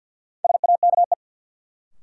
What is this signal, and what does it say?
Morse code, heard as a keyed tone, about 52 wpm, 700 Hz. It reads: SAKE
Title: Sets of 4 Words - New Word Drill - 50wpm